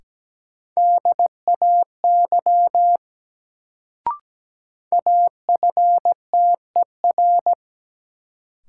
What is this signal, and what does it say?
Morse code, heard as a keyed tone, about 17 wpm, 700 Hz. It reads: DAY AFTER
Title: Sentences from Top 200 Words - New Word Drill - 17wpm